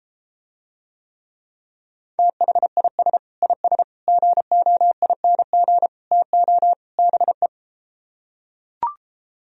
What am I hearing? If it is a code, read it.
Morse code, heard as a keyed tone, about 33 wpm, 700 Hz. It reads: THIS IS GOING TO BE